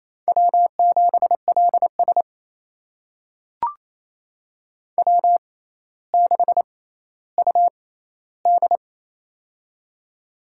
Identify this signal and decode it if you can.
Morse code, heard as a keyed tone, about 28 wpm, spaced as for 14 wpm, 700 Hz. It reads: W7LS W6UD
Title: USA Call Signs - 4 Characters - 6x Character Spacing - 28wpm